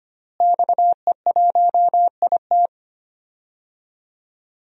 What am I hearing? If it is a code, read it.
Morse code, heard as a keyed tone, about 25 wpm, 700 Hz. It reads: XE1IT
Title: DXCC-100 Call Signs - 5 Characters - Repeated 2x - 25wpm